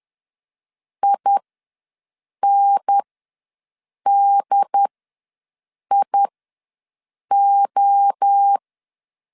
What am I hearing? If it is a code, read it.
Morse code, heard as a keyed tone, about 10 wpm, spaced as for 6 wpm, 800 Hz. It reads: INDIO
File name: LEZIONI / 6 PKX0 / 6 PK ws 10wpm 3745